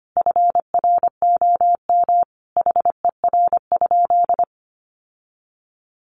Morse code, heard as a keyed tone, about 25 wpm, 700 Hz. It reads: FROM HER?
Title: Sentences from Top 1400 Words 25wpm